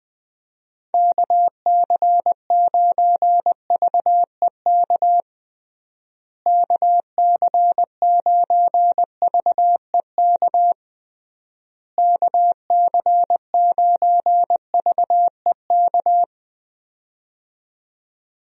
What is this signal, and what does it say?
Morse code, heard as a keyed tone, about 20 wpm, 700 Hz. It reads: KC9VEK KC9VEK KC9VEK
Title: USA Call Signs - 6 Characters - Repeated 3x - 20wpm